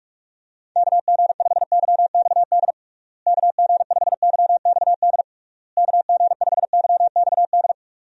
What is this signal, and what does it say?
Morse code, heard as a keyed tone, about 41 wpm, 700 Hz. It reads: KG5YXD KG5YXD KG5YXD
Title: USA Call Signs - 6 Characters - Repeated 3x - 45wpm